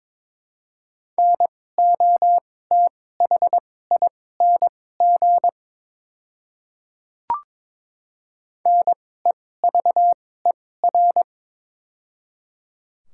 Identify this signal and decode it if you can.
Morse code, heard as a keyed tone, about 22 wpm, spaced as for 16 wpm, 700 Hz. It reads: NOTHING NEVER
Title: 2x Character Spacing - Top 500 Words - 22wpm